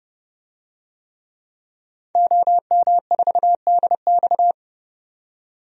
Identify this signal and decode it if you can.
Morse code, heard as a keyed tone, about 31 wpm, 700 Hz. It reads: OM4DX
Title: DXCC-100 Call Signs - 5 Characters - 30wpm